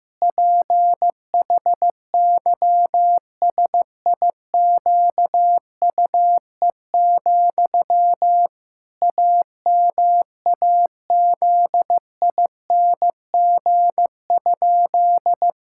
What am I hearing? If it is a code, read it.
Morse code, heard as a keyed tone, about 15 wpm, 700 Hz. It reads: PHYSIQUE, AMAZING?
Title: Pangram Challenge v2 - 15wpm